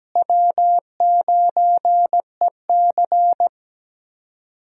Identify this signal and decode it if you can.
Morse code, heard as a keyed tone, about 17 wpm, 700 Hz. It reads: W9EC
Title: USA Call Signs - 4 Characters - Repeated 2x - 17wpm